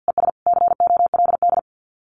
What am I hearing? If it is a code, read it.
Morse code, heard as a keyed tone, about 50 wpm, 700 Hz. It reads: ES COLD
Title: Common QSO Phrases 50wpm